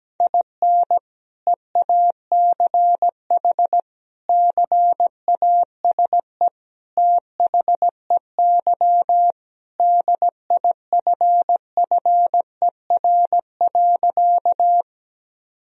Morse code, heard as a keyed tone, about 17 wpm, 700 Hz. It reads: IN EACH CASE THEY DIFFER.
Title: Sentences from Top 1000 Words - Review Part 1 - 17wpm